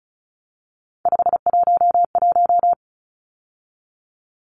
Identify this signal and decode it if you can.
Morse code, heard as a keyed tone, about 34 wpm, 700 Hz. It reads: 511
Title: Numbers - 3-Digit 35wpm